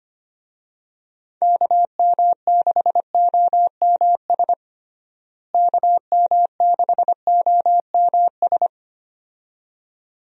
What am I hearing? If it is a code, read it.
Morse code, heard as a keyed tone, about 25 wpm, 700 Hz. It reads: KM6OMS KM6OMS
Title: USA Call Signs - 6 Characters - Repeated 2x - 25wpm